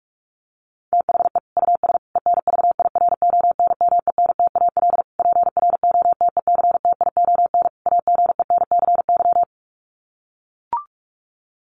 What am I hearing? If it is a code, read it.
Morse code, heard as a keyed tone, about 44 wpm, 700 Hz. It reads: THE US ENVIRONMENTAL PROTECTION AGENCY
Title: Sets of 5 Words Encore2 45wpm